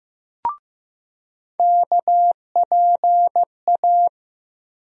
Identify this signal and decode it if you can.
Morse code, heard as a keyed tone, about 15 wpm, 700 Hz. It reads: KPA
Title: USA Call Sign Postfixes - 3 Letters 15wpm